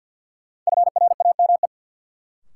Morse code, heard as a keyed tone, about 53 wpm, 700 Hz. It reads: FRAME
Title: Sentences from Top 1400 Words - New Word Drill Alt - 50wpm